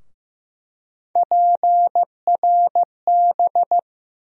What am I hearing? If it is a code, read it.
Morse code, heard as a keyed tone, about 15 wpm, 700 Hz. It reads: PRB
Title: USA Call Sign Postfixes - 3 Letters 15wpm